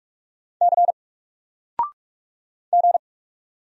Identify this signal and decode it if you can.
Morse code, heard as a keyed tone, about 44 wpm, 700 Hz. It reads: C G
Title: Single Letter-Number 45wpm